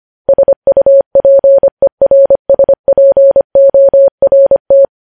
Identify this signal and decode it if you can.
Morse code, heard as a keyed tone, about 25 wpm, 550 Hz. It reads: SUPERSPORT
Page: News Headlines in Morse Code at 25 WPM